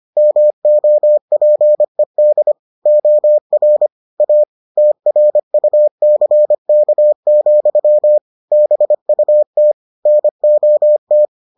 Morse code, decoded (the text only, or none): MOPED OR A TRUCK, BUT NOT